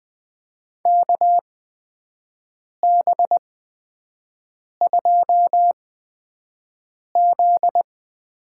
Morse code, heard as a keyed tone, about 20 wpm, spaced as for 5 wpm, 700 Hz. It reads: KB2Z
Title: USA Call Signs - 6 Characters - 8x Character Spacing - 20wpm